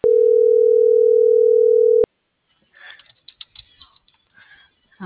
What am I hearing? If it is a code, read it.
Ringback tone (North America), 440 + 480 Hz.